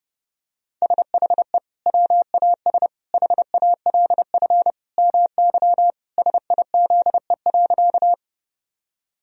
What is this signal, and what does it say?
Morse code, heard as a keyed tone, about 29 wpm, 700 Hz. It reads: SHE WAS HALF MY SIZE.
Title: Sentences from Top 1000 Words - Review Part 2 - 30wpm